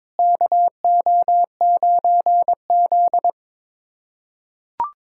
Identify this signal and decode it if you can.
Morse code, heard as a keyed tone, about 22 wpm, 700 Hz. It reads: KO9Z
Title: USA Callsign DE Callsign 22wpm